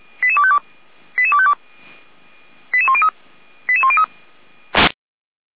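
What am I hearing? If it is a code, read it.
Five-tone selective call (ZVEI), heard as quick runs of five tones, tones of about 70 ms. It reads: 88252 88252 88103 88103
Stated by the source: ZVEI five-tone selective call 88191 received on SDR.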